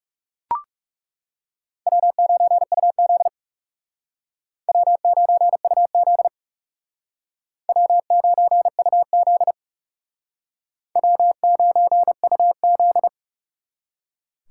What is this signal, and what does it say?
Morse code, heard as a keyed tone, about 40 wpm, 700 Hz. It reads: W9UZ W9UZ W9UZ W9UZ